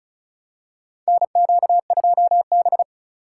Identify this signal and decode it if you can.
Morse code, heard as a keyed tone, about 36 wpm, 700 Hz. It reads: NQ2B
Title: USA Call Signs - 4 Characters - 4x Character Spacing - 35wpm